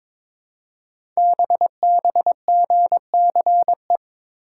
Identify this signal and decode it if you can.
Morse code, heard as a keyed tone, about 23 wpm, 700 Hz. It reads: BBGCE
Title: Code Groups - 5 Letter:Numbers - 22wpm